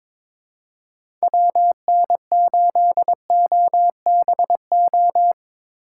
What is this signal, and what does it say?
Morse code, heard as a keyed tone, about 23 wpm, 700 Hz. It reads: WN8OBO
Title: USA Call Signs - 6 Characters - 8x Character Spacing - 22wpm